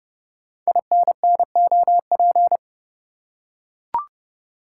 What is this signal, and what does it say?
Morse code, heard as a keyed tone, about 28 wpm, 700 Hz. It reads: INNOP
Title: Code Groups - 5 Letters - 30wpm